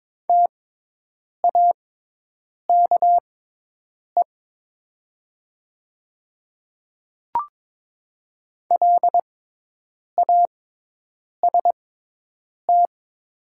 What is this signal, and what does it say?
Morse code, heard as a keyed tone, about 22 wpm, spaced as for 8 wpm, 700 Hz. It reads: TAKE LAST
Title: 6x Character Spacing - Top 500 Words - 22wpm